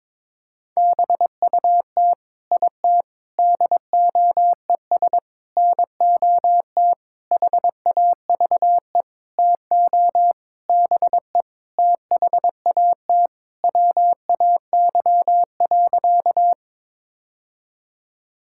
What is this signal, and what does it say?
Morse code, heard as a keyed tone, about 22 wpm, 700 Hz. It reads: BUT IT DOES NOT HAVE TO BE THAT WAY.
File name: practice / 22wpm / Sentences with 8-Word Phrases - 22wpm